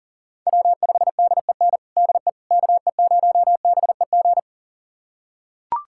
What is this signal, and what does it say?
Morse code, heard as a keyed tone, about 40 wpm, 700 Hz. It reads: W5DEN DE KE0BEG
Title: USA Callsign DE Callsign 40wpm